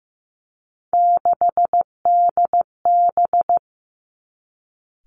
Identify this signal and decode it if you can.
Morse code, heard as a keyed tone, about 15 wpm, 700 Hz. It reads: 6DB
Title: Three Challenging Letter-Number Combinations 15wpm